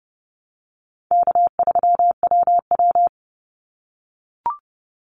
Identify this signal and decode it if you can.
Morse code, heard as a keyed tone, about 30 wpm, 700 Hz. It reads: K3WW
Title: Easy Call Signs - 30wpm